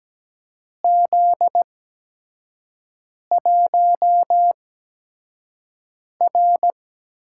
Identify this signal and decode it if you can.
Morse code, heard as a keyed tone, about 17 wpm, spaced as for 5 wpm, 700 Hz. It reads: Z1R
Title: USA Call Signs - 8x Character Spacing - 17wpm